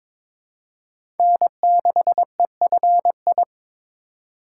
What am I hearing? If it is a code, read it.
Morse code, heard as a keyed tone, about 22 wpm, 700 Hz. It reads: N6EFI
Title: USA Call Signs - 5 Characters - 3x Character Spacing - 22wpm